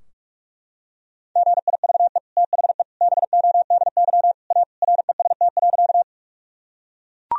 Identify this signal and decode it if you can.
Morse code, heard as a keyed tone, about 41 wpm, 700 Hz. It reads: GIVE THE BODY A REST.
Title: Sentences from Top 1000 Words - Review Part 2 - 45wpm